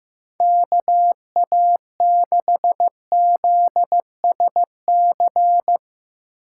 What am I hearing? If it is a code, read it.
Morse code, heard as a keyed tone, about 15 wpm, 700 Hz. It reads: KA6ZSC
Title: USA Call Signs - 4x Character Spacing - 15wpm